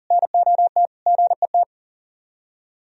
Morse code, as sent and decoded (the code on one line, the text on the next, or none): -. --- - / --. -
NOT GET